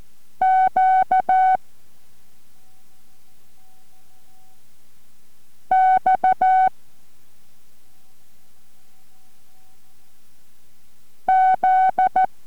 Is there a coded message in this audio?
Yes: Morse code from 0.4 to 12.3 s, heard as a keyed tone, about 14 wpm, 750 Hz. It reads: Q X Z